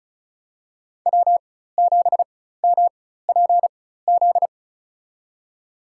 Morse code, heard as a keyed tone, about 36 wpm, spaced as for 16 wpm, 700 Hz. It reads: W7MPZ
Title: USA Call Signs - 5 Characters - 4x Character Spacing - 35wpm